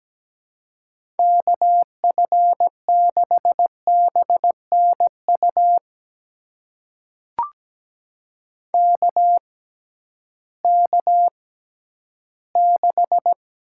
Morse code, heard as a keyed tone, about 17 wpm, spaced as for 11 wpm, 700 Hz. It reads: KF6BNU KK6
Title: USA Call Signs - 6 Characters - 6x Character Spacing - 17wpm